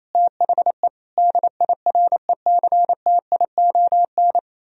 Morse code, heard as a keyed tone, about 35 wpm, 700 Hz. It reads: THE DIRECTION